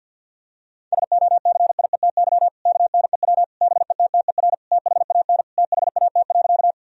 Morse code, heard as a keyed tone, about 50 wpm, 700 Hz. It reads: SOCIETY KNEW BETTER THAN THAT.
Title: Sentences from Top 1000 Words - Review Part 4 - 50wpm